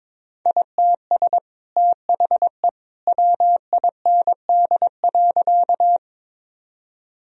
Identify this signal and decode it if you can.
Morse code, heard as a keyed tone, about 23 wpm, 700 Hz. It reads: ITS THE WIND.
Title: Sentences from Top 1000 Words - Review Part 2 - 22wpm